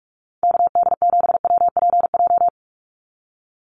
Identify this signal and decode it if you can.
Morse code, heard as a keyed tone, about 44 wpm, 700 Hz. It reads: KD7WPJ
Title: NA SOTA Activator Callsigns 45wpm